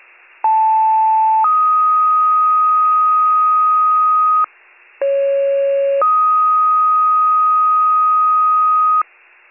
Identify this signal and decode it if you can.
Two-tone page, heard as a one-second tone, then a three-second tone, about 880 then 1250 Hz, about 550 then 1250 Hz. It is heard from 0.4 to 9.0 s.